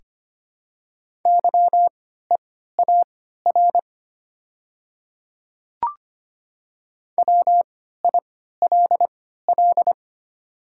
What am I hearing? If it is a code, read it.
Morse code, heard as a keyed tone, about 25 wpm, spaced as for 14 wpm, 690 Hz. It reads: YEAR WILL
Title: 3x Character Spacing - Top 200 Words - 25wpm